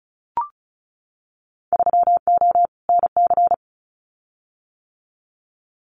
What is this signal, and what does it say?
Morse code, heard as a keyed tone, about 36 wpm, 700 Hz. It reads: 3O NC
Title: Winter Field Day Exchanges 35wpm